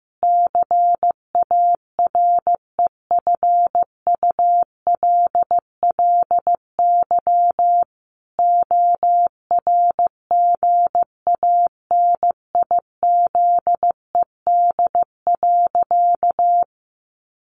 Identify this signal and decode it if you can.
Morse code, heard as a keyed tone, about 14 wpm, 700 Hz. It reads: CAREFULLY ORGANIZED.
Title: Sentences from Top 1300 Words 15wpm